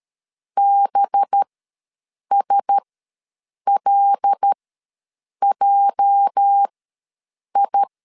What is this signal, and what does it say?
Morse code, heard as a keyed tone, about 13 wpm, spaced as for 7 wpm, 800 Hz. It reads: BSLJI
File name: LEZIONI / 8 YJ45 / 8 YJ ws 12wpm 3342